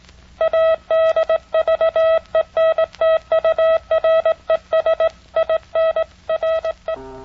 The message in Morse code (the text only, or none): ADVENTURES IN RE